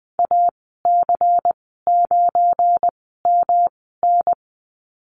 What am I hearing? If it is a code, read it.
Morse code, heard as a keyed tone, about 20 wpm, spaced as for 14 wpm, 700 Hz. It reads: AC9MN
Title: 193 Lesson 31g - Letter v - Call Signs 20wpm